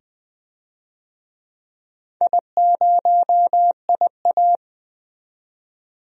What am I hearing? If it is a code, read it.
Morse code, heard as a keyed tone, about 20 wpm, 700 Hz. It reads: I0IA
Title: DXCC-100 Call Signs - 4 Characters - 20wpm